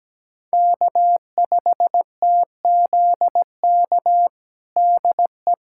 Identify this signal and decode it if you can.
Morse code, heard as a keyed tone, about 17 wpm, 700 Hz. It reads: K5TZK DE
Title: USA Callsign DE Callsign - 5 Characters - 17wpm